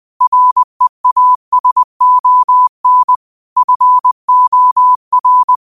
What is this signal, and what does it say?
Morse code, heard as a keyed tone, about 20 wpm, 1000 Hz. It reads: REASON FOR